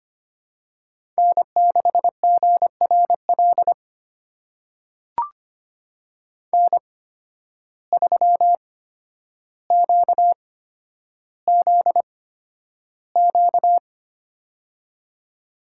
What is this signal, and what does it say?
Morse code, heard as a keyed tone, about 25 wpm, spaced as for 11 wpm, 700 Hz. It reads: N6GRL N3QZQ
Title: USA Call Signs - 5 Characters - 8x Character Spacing - 25wpm